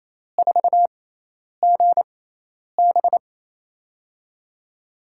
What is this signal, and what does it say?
Morse code, heard as a keyed tone, about 28 wpm, spaced as for 10 wpm, 700 Hz. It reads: 4GB
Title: USA Call Signs - 4 Characters - 6x Character Spacing - 28wpm